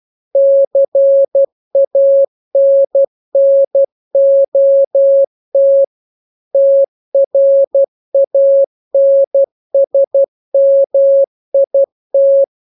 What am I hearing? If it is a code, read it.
Morse code, heard as a keyed tone, about 12 wpm, 550 Hz. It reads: CANNOT TRANSMIT